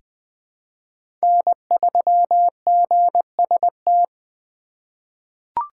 Morse code, heard as a keyed tone, about 20 wpm, 700 Hz. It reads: N3GST
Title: USA Call Signs - 5 Characters - 3x Character Spacing - 20wpm